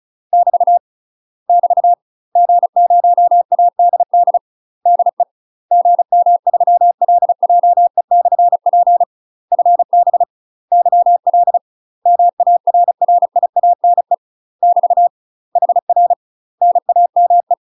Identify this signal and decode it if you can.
Morse code, heard as a keyed tone, about 34 wpm, 700 Hz. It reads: <BT> <BT> G0ADD DE GM3LJE/P FB YL MARRIANE <BT> HR NAME
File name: UK MASTER QSOs TRAINING / 35WPM / UK MASTER QSOs 0015